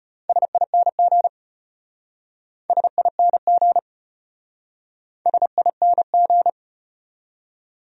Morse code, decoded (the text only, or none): SING SING SING